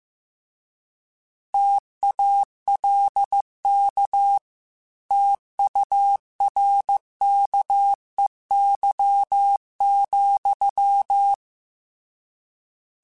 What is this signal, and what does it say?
Morse code, heard as a keyed tone, about 15 wpm, 800 Hz. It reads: TALK TURKEY,
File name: Misc / Misc-mp3s / Word Pairs @13